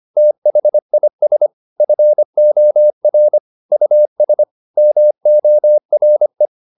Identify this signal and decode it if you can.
Morse code, heard as a keyed tone, about 25 wpm, 600 Hz. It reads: THIS FOR US MORE
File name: the old man and the sea part 1 25wpm